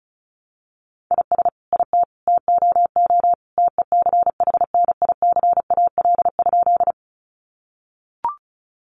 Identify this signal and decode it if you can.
Morse code, heard as a keyed tone, about 34 wpm, 700 Hz. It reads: IS IT TOO TECHNICAL?
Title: Sentences from Top 1100 Words 35wpm